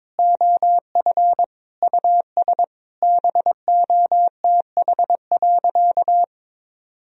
Morse code, heard as a keyed tone, about 22 wpm, 700 Hz. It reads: OF US BOTH.
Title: Sentences from Top 1000 Words - Review Part 4 - 22wpm